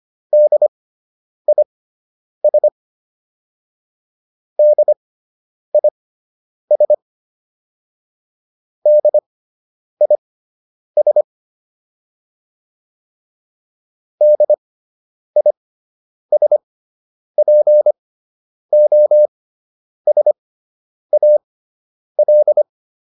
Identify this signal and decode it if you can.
Morse code, heard as a keyed tone, about 25 wpm, spaced as for 9 wpm, 600 Hz. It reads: DIS DIS DIS DISPOSAL